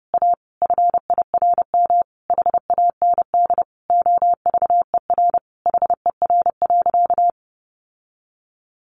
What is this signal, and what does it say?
Morse code, heard as a keyed tone, about 28 wpm, 700 Hz. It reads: A FIRM HAND OVER HER.
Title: Sentences from Top 1400 Words 30wpm